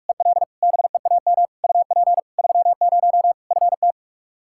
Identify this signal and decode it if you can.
Morse code, heard as a keyed tone, about 46 wpm, 700 Hz. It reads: EL BEAM UP 30 FT
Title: Practice QSO - Exchange 2 - Info - 45wpm